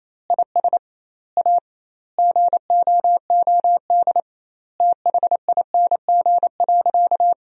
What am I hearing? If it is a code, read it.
Morse code, heard as a keyed tone, about 28 wpm, 700 Hz. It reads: IS A GOOD THING.
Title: 2x Word-Spacing - Sentences from Top 300 Words - 28wpm